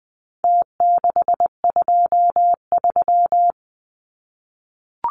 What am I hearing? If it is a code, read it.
Morse code, heard as a keyed tone, about 20 wpm, 700 Hz. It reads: T623
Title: Numbers - Contest Style - 4-Digit 20wpm